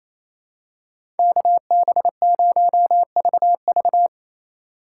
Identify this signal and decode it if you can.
Morse code, heard as a keyed tone, about 28 wpm, 700 Hz. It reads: KB0VV
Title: USA Call Signs - 4x Character Spacing - 28wpm